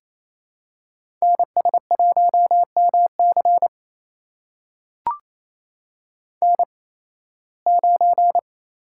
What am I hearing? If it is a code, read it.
Morse code, heard as a keyed tone, about 27 wpm, spaced as for 18 wpm, 700 Hz. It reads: NS1MC N9
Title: USA Call Signs - 8x Character Spacing - 28wpm